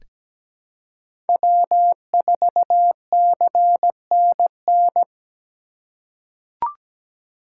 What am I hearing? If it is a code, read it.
Morse code, heard as a keyed tone, about 17 wpm, 700 Hz. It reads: W4CNN
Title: USA Call Signs - 5 Characters - 3x Character Spacing - 17wpm